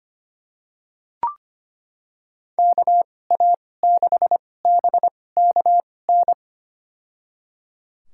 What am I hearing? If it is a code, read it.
Morse code, heard as a keyed tone, about 25 wpm, spaced as for 18 wpm, 700 Hz. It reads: KA6BKN
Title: USA Call Signs - 2x Character Spacing - 25wpm